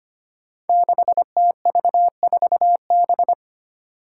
Morse code, heard as a keyed tone, about 25 wpm, 700 Hz. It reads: 6TV4B
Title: Five Challenging Letter-Number Combinations 25wpm